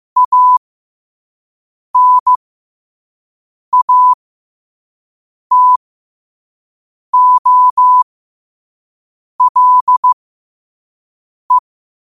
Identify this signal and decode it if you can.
Morse code, heard as a keyed tone, about 15 wpm, spaced as for 5 wpm, 1000 Hz. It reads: ANATOLE
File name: qotdfiles / 5wpm